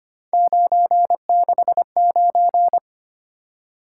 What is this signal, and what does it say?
Morse code, heard as a keyed tone, about 25 wpm, 700 Hz. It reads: 969
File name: Three Letter-Number Combinations - 25wpm